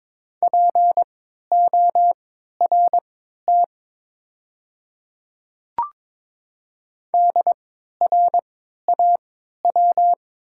Morse code, heard as a keyed tone, about 22 wpm, spaced as for 12 wpm, 700 Hz. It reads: PORT DRAW
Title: 3x Character Spacing - Top 300 Words - 22wpm